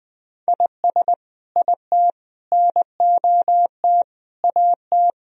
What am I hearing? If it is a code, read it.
Morse code, heard as a keyed tone, about 21 wpm, 700 Hz. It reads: IS IT NOT AT
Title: Sentences from Top 1000 Words - Review Part 4 - 20wpm